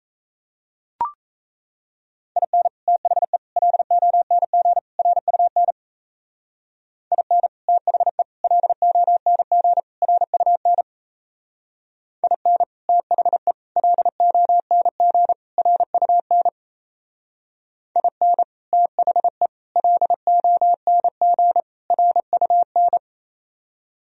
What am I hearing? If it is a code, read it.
Morse code, heard as a keyed tone, about 34 wpm, 700 Hz. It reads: IN THE LONG RUN IN THE LONG RUN IN THE LONG RUN IN THE LONG RUN
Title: Speed-Racing - Sets of 4 Words 28wpm